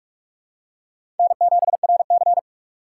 Morse code, heard as a keyed tone, about 47 wpm, 700 Hz. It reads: N7RC
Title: USA Call Signs - 4 Characters - 6x Character Spacing - 45wpm